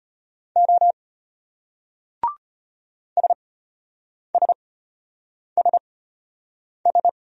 Morse code, heard as a keyed tone, about 34 wpm, spaced as for 8 wpm, 700 Hz. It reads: O SSSS